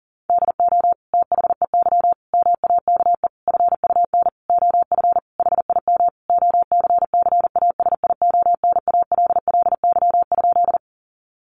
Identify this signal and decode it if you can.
Morse code, heard as a keyed tone, about 40 wpm, 700 Hz. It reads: DO THEY MAKE FUN OF HIM OCCASIONALLY?
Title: Sentences from Top 1300 Words 40wpm